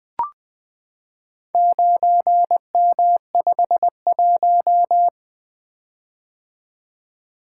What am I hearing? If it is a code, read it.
Morse code, heard as a keyed tone, about 20 wpm, 700 Hz. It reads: 9M51
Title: Four Letter-Number Combinations - 20wpm